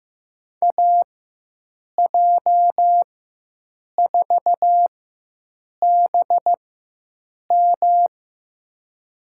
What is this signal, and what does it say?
Morse code, heard as a keyed tone, about 15 wpm, spaced as for 7 wpm, 700 Hz. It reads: AJ4BM
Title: USA Call Signs - 5 Characters - 4x Character Spacing - 15wpm